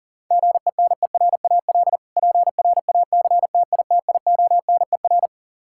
Morse code, heard as a keyed tone, about 40 wpm, 700 Hz. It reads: GENERAL PRACTITIONER
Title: Practice QSO - Exchange 3 - More Info - 40wpm